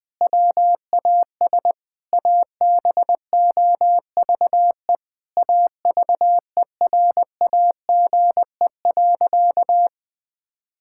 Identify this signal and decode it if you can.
Morse code, heard as a keyed tone, about 20 wpm, 700 Hz. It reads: WAS ABOVE AVERAGE.
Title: Sentences from Top 1000 Words - Review Part 6 - 20wpm